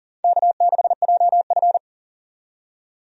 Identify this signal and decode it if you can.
Morse code, heard as a keyed tone, about 40 wpm, 700 Hz. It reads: K6JF
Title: USA Call Signs - 4 Characters - 40wpm